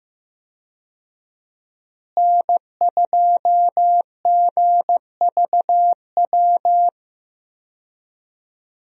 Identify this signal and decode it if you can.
Morse code, heard as a keyed tone, about 15 wpm, 700 Hz. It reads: N2GVW
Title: USA Call Signs - 5 Characters - Repeated 3x - 15wpm